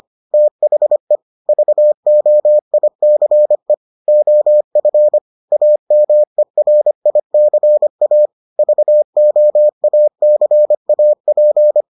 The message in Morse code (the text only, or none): THE VOICE OF AMERICA VOACAP